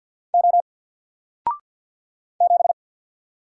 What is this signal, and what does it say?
Morse code, heard as a keyed tone, about 50 wpm, 700 Hz. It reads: O 7
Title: Single Letter-Number 50wpm